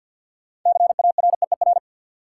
Morse code, heard as a keyed tone, about 53 wpm, 700 Hz. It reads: CAREER
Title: Sentences from Top 1000 Words - New Word Drill - 50wpm